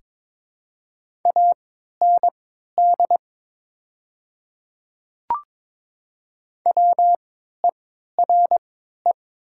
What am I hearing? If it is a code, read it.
Morse code, heard as a keyed tone, about 22 wpm, spaced as for 12 wpm, 700 Hz. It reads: AND WERE
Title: 3x Character Spacing - Top 100 Words - 22wpm